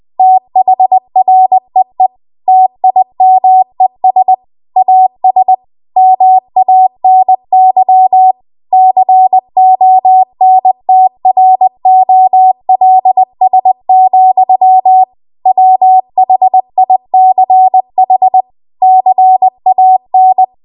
Morse code, heard as a keyed tone, about 20 wpm, 750 Hz. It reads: THREE TIMES AS MANY CONTROLS, WHICH CAN